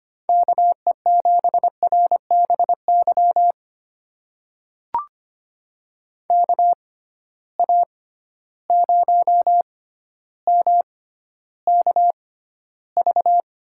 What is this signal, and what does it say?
Morse code, heard as a keyed tone, about 25 wpm, spaced as for 13 wpm, 700 Hz. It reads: KE7RBY KA0MKV
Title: USA Call Signs - 6 Characters - 6x Character Spacing - 25wpm